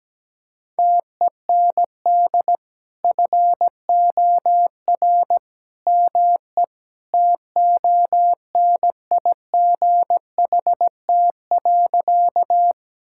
Morse code, heard as a keyed tone, about 17 wpm, 700 Hz. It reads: TEND FOR ME TONIGHT.
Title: Sentences from Top 1000 Words - Review Part 5 - 17wpm